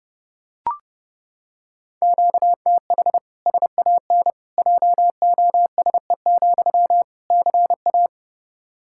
Morse code, heard as a keyed tone, about 30 wpm, 700 Hz. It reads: QTH SAN JOSE, CA
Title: Practice QSO - Exchange 1 - Beginning - 30wpm